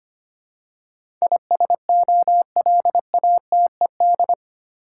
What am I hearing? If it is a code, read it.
Morse code, heard as a keyed tone, about 25 wpm, 700 Hz. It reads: ISOLATED
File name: Eight Letter Words - 25wpm